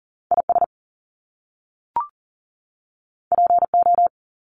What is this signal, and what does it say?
Morse code, heard as a keyed tone, about 40 wpm, 700 Hz. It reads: IS PO
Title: Top Two Letter Combinations in Words 40wpm